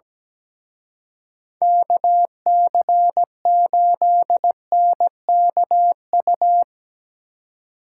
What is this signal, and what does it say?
Morse code, heard as a keyed tone, about 19 wpm, 700 Hz. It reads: KC8NKU
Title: Speed Racing - USA 2x3 Call Signs 17wpm